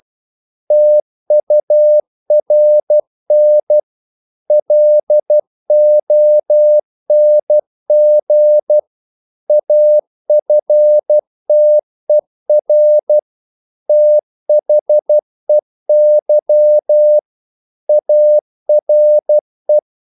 Morse code, heard as a keyed tone, about 12 wpm, 600 Hz. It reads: TURN LONG AFTER THEY ARE
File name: CW / Practice / If12